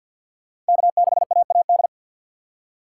Morse code, heard as a keyed tone, about 53 wpm, 700 Hz. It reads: K6AAD
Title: USA Call Signs - 5 Characters - 4x Character Spacing - 50wpm